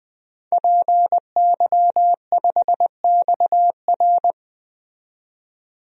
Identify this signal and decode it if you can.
Morse code, heard as a keyed tone, about 20 wpm, 700 Hz. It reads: PY5XR
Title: DXCC-100 Call Signs - 5 Characters - 20wpm